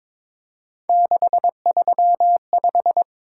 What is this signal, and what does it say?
Morse code, heard as a keyed tone, about 26 wpm, 700 Hz. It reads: 635
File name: Speed-Racing - Three Letter-Number Combinations - 22wpm